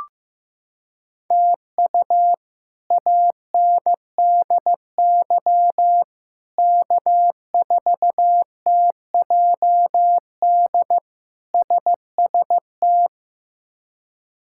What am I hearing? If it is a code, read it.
Morse code, heard as a keyed tone, about 15 wpm, 700 Hz. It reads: TU ANDY K4TJD SST
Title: K1USN SST QSO Practice - 15wpm